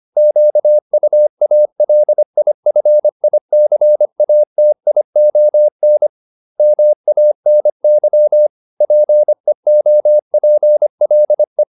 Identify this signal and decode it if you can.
Morse code, heard as a keyed tone, about 25 wpm, 600 Hz. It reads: QUALIFICATION MANY PEOPLE